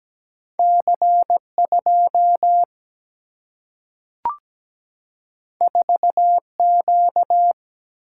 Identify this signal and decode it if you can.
Morse code, heard as a keyed tone, about 17 wpm, 700 Hz. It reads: C 2 4 Q
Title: Single-Letter Single-Number 17wpm